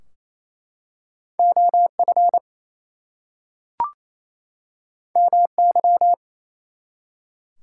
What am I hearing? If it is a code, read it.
Morse code, heard as a keyed tone, about 29 wpm, 700 Hz. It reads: OF MY
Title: Two Letter Words - 28wpm